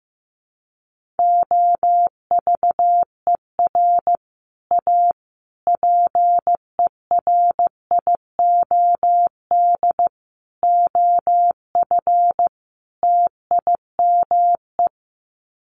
Morse code, heard as a keyed tone, about 15 wpm, 700 Hz. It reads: OVER A PERIOD OF TIME
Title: Sets of 5 Words Encore2 15wpm